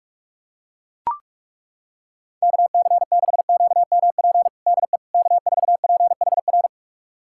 Morse code, heard as a keyed tone, about 42 wpm, 700 Hz. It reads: KC6QMP DE K4PHR
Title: USA Callsign DE Callsign 45wpm